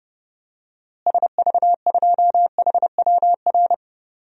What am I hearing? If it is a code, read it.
Morse code, heard as a keyed tone, about 32 wpm, 700 Hz. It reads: SV2HWR
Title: DXCC-100 Call Signs - 6 Characters - Repeated 2x - 30wpm